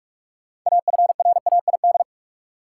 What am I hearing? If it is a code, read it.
Morse code, heard as a keyed tone, about 45 wpm, 700 Hz. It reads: AFRAID
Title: Sentences from Top 1500 Words - New Word Drill Alt - 45wpm